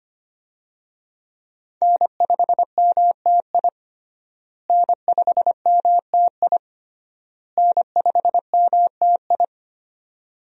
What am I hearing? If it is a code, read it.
Morse code, heard as a keyed tone, about 25 wpm, 700 Hz. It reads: N5MTI N5MTI N5MTI
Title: USA Call Signs - 5 Characters - Repeated 3x - 25wpm